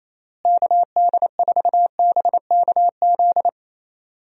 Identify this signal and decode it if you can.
Morse code, heard as a keyed tone, about 28 wpm, 700 Hz. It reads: KD4BKZ
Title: USA Call Signs - 6 Characters - Repeated 3x - 28wpm